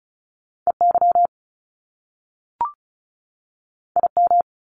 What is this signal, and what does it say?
Morse code, heard as a keyed tone, about 36 wpm, 700 Hz. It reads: EY IM
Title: Top Two Letter Combinations in Words 35wpm